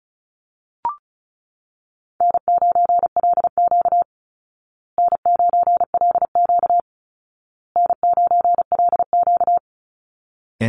Morse code, heard as a keyed tone, about 34 wpm, 700 Hz. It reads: N9LQ N9LQ N9LQ
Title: Easy Call Signs Repeated 3x - 35wpm